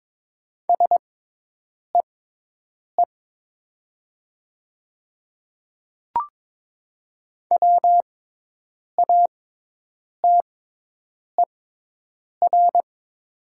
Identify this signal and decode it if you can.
Morse code, heard as a keyed tone, about 22 wpm, spaced as for 8 wpm, 700 Hz. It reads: SEE WATER
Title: 6x Character Spacing - Top 300 Words - 22wpm